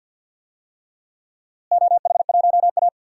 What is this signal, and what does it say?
Morse code, heard as a keyed tone, about 49 wpm, 700 Hz. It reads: OH1A
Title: DXCC-100 Call Signs - 4 Characters - Repeated 3x - 50wpm